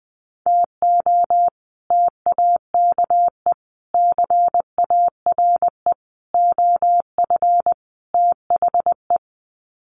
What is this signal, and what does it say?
Morse code, heard as a keyed tone, about 20 wpm, 700 Hz. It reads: TO TAKE CARE OF THE
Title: Sets of 5 Words Encore 20wpm